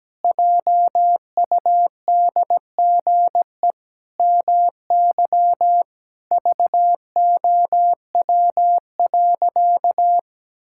Morse code, heard as a keyed tone, about 17 wpm, 700 Hz. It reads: JUDGE MY VOW.
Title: Pangram Challenge v2 - 17wpm